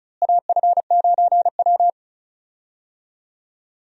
Morse code, heard as a keyed tone, about 36 wpm, 700 Hz. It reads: AF9W
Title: USA Call Signs - 4 Characters - Repeated 2x - 35wpm